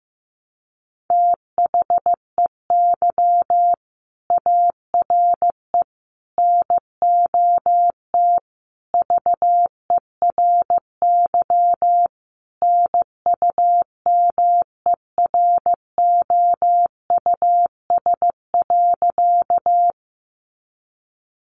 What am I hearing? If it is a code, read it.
Morse code, heard as a keyed tone, about 15 wpm, 700 Hz. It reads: THEY ARE NOT VERY NUMEROUS.
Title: Sentences from Top 1400 Words 15wpm